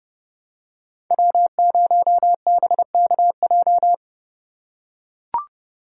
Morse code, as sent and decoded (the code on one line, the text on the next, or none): .-- ----- -... -.- .---
W0BKJ